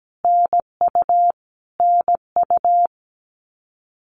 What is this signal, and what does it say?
Morse code, heard as a keyed tone, about 17 wpm, 700 Hz. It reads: NU NU
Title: Canadian Provinces and Territories - Repeated 3x - 17wpm